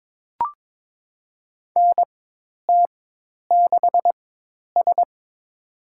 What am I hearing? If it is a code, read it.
Morse code, heard as a keyed tone, about 23 wpm, spaced as for 10 wpm, 700 Hz. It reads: NT6S
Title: USA Call Signs - 4 Characters - 4x Character Spacing - 22wpm